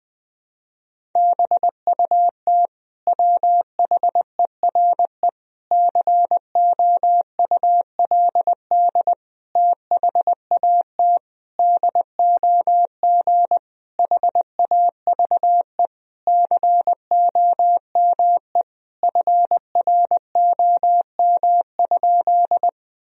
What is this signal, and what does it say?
Morse code, heard as a keyed tone, about 21 wpm, 700 Hz. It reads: BUT WHERE COULD THAT DOG HAVE COME FROM?
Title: Sentences from Top 1000 Words - Review Part 2 - 20wpm